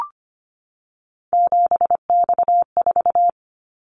Morse code, heard as a keyed tone, about 25 wpm, 700 Hz. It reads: 7X4
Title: Callsign Prefixes for Top 100 DXCC - 25wpm